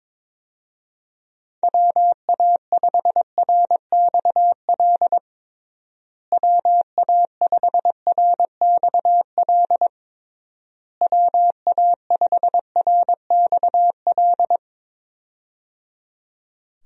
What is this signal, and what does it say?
Morse code, heard as a keyed tone, about 22 wpm, 700 Hz. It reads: WA5RXL WA5RXL WA5RXL
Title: USA Call Signs - 6 Characters - Repeated 3x - 22wpm